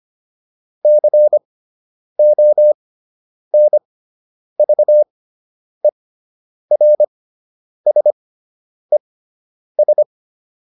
Morse code, heard as a keyed tone, about 25 wpm, spaced as for 9 wpm, 600 Hz. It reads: CONVERSES